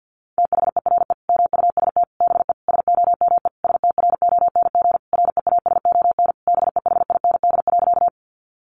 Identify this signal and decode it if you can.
Morse code, heard as a keyed tone, about 50 wpm, 700 Hz. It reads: THERE MUST BE SOME STRONG REASON BEHIND.
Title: Sentences from Top 700 Words 50wpm